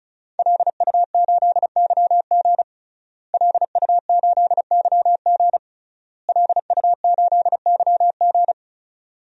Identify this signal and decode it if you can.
Morse code, heard as a keyed tone, about 34 wpm, 700 Hz. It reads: LU8YG LU8YG LU8YG
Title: DXCC-100 Call Signs - 5 Characters - Repeated 3x - 35wpm